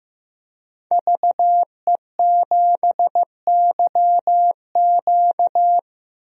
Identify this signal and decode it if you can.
Morse code, heard as a keyed tone, about 15 wpm, 700 Hz. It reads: VE7YQ
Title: DXCC-100 Call Signs - 5 Characters - 15wpm